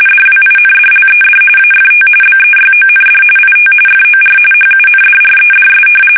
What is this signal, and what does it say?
RTTY, about 45.45 baud. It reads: DWINDLED AFT BEYOND IT FOR HUNDREDS
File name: fsk850 / battleshipnj